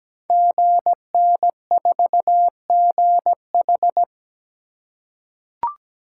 Morse code, heard as a keyed tone, about 17 wpm, 700 Hz. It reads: GN4GH
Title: Code Groups - 5 Letter:Numbers - 17wpm